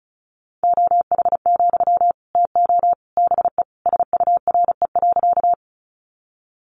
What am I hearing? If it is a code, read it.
Morse code, heard as a keyed tone, about 34 wpm, 700 Hz. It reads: OH, TO BE SURE.
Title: Sentences from Top 500 Words 35wpm